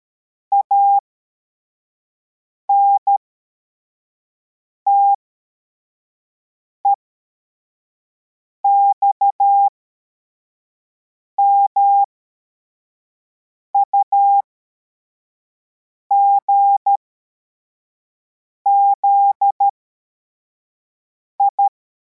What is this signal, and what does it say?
Morse code, heard as a keyed tone, about 13 wpm, spaced as for 4 wpm, 800 Hz. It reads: ANTEX MUGZI
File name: formation / cw / Text43